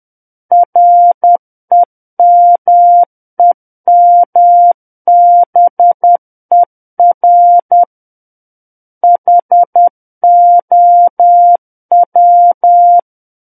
Morse code, heard as a keyed tone, about 10 wpm, 700 Hz. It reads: REMEMBER HOW
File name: Book One / 01-Chapter 1 - 10 WPM